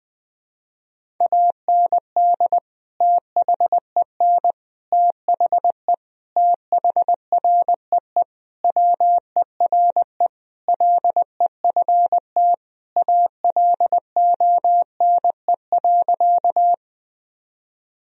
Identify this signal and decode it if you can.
Morse code, heard as a keyed tone, about 20 wpm, 700 Hz. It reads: AND THEN THE THREE WERE LEFT ALONE.
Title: Sentences from Top 1000 Words - Review Part 1 - 20wpm